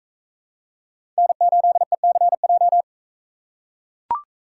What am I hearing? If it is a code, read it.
Morse code, heard as a keyed tone, about 40 wpm, 700 Hz. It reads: N8ECJ